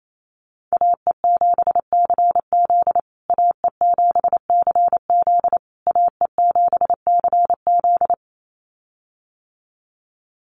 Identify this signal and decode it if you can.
Morse code, heard as a keyed tone, about 28 wpm, 700 Hz. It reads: AE7CZ AE7CZ AE7CZ
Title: USA Call Signs Repeated 3x 28wpm